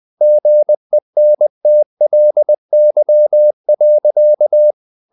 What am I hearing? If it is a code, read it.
Morse code, heard as a keyed tone, about 20 wpm, 600 Hz. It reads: GENTLY.
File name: the old man and the sea / the old man and the sea part 1 20wpm